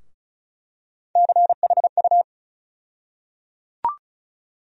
Morse code, heard as a keyed tone, about 36 wpm, 700 Hz. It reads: CHU
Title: USA Call Sign Postfixes - 3 Letters 35wpm